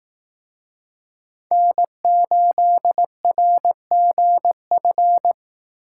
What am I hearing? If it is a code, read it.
Morse code, heard as a keyed tone, about 18 wpm, 700 Hz. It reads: N8RGF
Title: Speed Racing - USA 1x3 Call Signs 15wpm